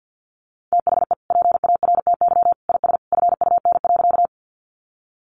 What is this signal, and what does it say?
Morse code, heard as a keyed tone, about 50 wpm, 700 Hz. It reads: THE PARTY IS FUN.
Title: Easy Phrases 50wpm